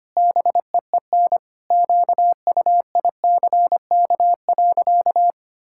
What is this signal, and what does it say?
Morse code, heard as a keyed tone, about 25 wpm, 700 Hz. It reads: BEEN QUICK.
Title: Sentences from Top 1000 Words - Review Part 5 - 25wpm